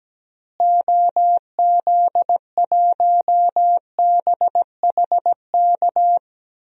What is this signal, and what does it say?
Morse code, heard as a keyed tone, about 17 wpm, 700 Hz. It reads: OZ1BHK
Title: DXCC-100 Call Signs - 6 Characters - Repeated 3x - 17wpm